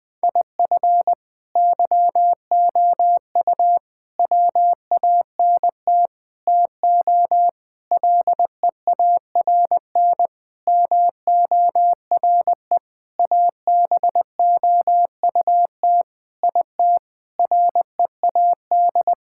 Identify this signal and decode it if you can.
Morse code, heard as a keyed tone, about 20 wpm, 700 Hz. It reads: IF YOU WANT TO LEARN MORE ABOUT IT READ
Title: Sentences with 7-Word Phrases - Spoken Beginning - 20wpm